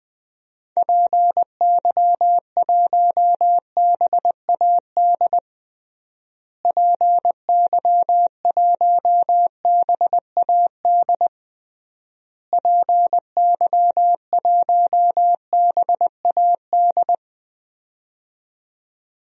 Morse code, heard as a keyed tone, about 20 wpm, 700 Hz. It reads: PY1BAD PY1BAD PY1BAD
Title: DXCC-100 Call Signs - 6 Characters - Repeated 3x - 20wpm